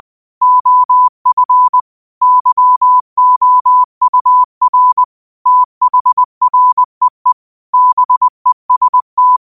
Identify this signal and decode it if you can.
Morse code, heard as a keyed tone, about 20 wpm, 1000 Hz. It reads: OF YOUR THREE BEST